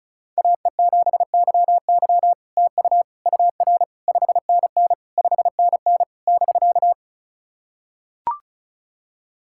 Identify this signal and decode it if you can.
Morse code, heard as a keyed tone, about 36 wpm, 700 Hz. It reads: AE7YY TU UR 5NN 5NN <BK>
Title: SOTA QSO Practice 35wpm